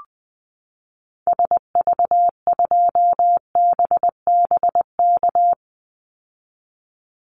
Morse code, heard as a keyed tone, about 19 wpm, 700 Hz. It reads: SV2BBK
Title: Call Signs Repeated 2x - 20wpm